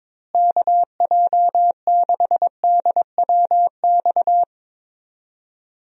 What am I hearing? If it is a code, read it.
Morse code, heard as a keyed tone, about 22 wpm, 700 Hz. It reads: KJ6DWX
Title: USA Call Signs - 6 Characters - Repeated 3x - 22wpm